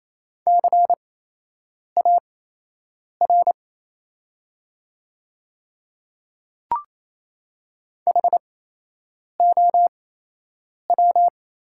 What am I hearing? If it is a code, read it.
Morse code, heard as a keyed tone, about 28 wpm, spaced as for 8 wpm, 700 Hz. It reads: CAR HOW
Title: 8x Character Spacing - Top 300 Words - 28wpm